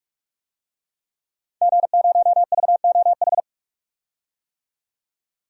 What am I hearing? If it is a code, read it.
Morse code, heard as a keyed tone, about 41 wpm, 700 Hz. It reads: G0VOH
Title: DXCC-100 Call Signs - 5 Characters - 45wpm